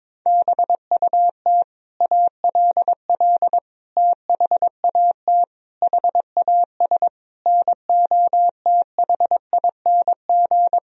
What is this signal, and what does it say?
Morse code, heard as a keyed tone, about 22 wpm, 700 Hz. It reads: BUT ALL THAT HAS NOTHING
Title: Sentences from Top 1000 Words - Review Part 3 - 22wpm